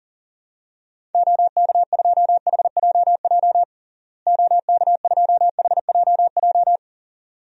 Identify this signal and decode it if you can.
Morse code, heard as a keyed tone, about 37 wpm, 700 Hz. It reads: OK2HJJ OK2HJJ
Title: DXCC-100 Call Signs - 6 Characters - Repeated 2x - 40wpm